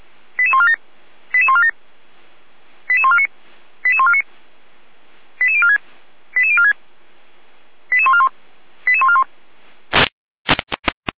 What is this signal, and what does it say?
Five-tone selective call (ZVEI), heard as quick runs of five tones, tones of about 70 ms. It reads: 88147 88147 88149 88149 88046 88046 88141 88141